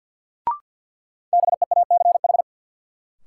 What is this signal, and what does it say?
Morse code, heard as a keyed tone, about 53 wpm, 700 Hz. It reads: BEACH
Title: Sentences from Top 1500 Words - New Word Drill - 50wpm